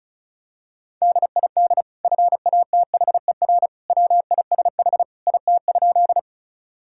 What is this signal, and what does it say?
Morse code, heard as a keyed tone, about 36 wpm, 700 Hz. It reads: DID FATHER WISH IT?
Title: Sentences from Top 1600 Words 35wpm